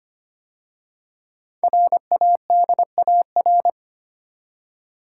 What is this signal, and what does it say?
Morse code, heard as a keyed tone, about 25 wpm, 700 Hz. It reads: RADAR
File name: 5 Letter Words - First 3 Letters Given - 25wpm